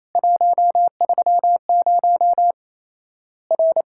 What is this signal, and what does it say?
Morse code, heard as a keyed tone, about 27 wpm, 700 Hz. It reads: 130 R
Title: Worked All Europe DX Contest - QTC Exchange - 28wpm